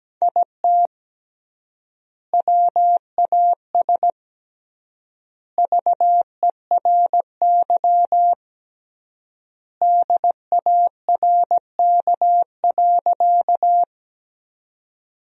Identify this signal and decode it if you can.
Morse code, heard as a keyed tone, about 17 wpm, 700 Hz. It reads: IT WAS VERY DARK.
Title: 3x Word-Spacing - Sentences from Top 500 Words - 17wpm